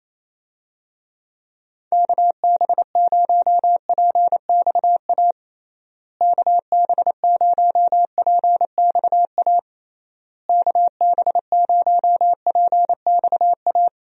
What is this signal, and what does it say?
Morse code, heard as a keyed tone, about 28 wpm, 700 Hz. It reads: KB0PXA KB0PXA KB0PXA
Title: USA Call Signs - 6 Characters - Repeated 3x - 28wpm